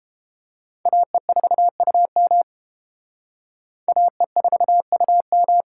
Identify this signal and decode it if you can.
Morse code, heard as a keyed tone, about 28 wpm, 700 Hz. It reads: AE4UM AE4UM